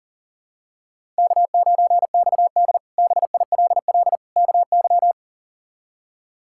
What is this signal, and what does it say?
Morse code, heard as a keyed tone, about 40 wpm, 700 Hz. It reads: K9XD BILL KY
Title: K1USN SST QSO Practice - 40wpm